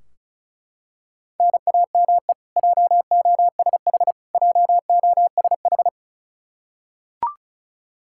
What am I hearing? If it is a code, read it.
Morse code, heard as a keyed tone, about 34 wpm, 700 Hz. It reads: NAME JOSH JOSH
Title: K1USN SST QSO Practice 35wpm